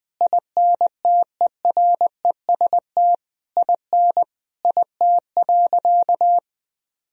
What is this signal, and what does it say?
Morse code, heard as a keyed tone, about 20 wpm, 700 Hz. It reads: INTEREST IN IT.
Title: Sentences from Top 1000 Words - Review Part 3 - 20wpm